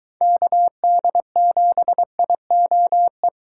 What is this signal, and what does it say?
Morse code, heard as a keyed tone, about 23 wpm, 700 Hz. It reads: KD7IOE